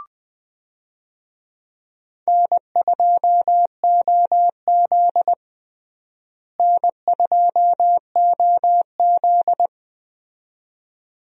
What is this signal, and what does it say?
Morse code, heard as a keyed tone, about 20 wpm, 700 Hz. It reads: N2OZ N2OZ
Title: USA Call Signs - 4 Characters - Repeated 2x - 20wpm